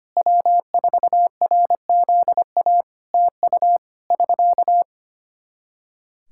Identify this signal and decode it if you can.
Morse code, heard as a keyed tone, about 25 wpm, 700 Hz. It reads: W4RZA TU <SK>
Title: Practice QSO - Exchange 4 - Ending - 25wpm